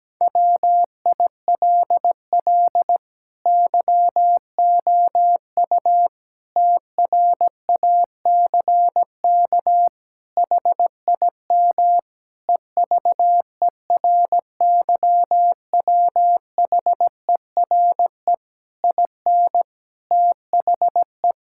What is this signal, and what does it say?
Morse code, heard as a keyed tone, about 17 wpm, 700 Hz. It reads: WILL YOU TRACK HIM EVERYWHERE IN THE
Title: Sentences from Top 1600 Words 17wpm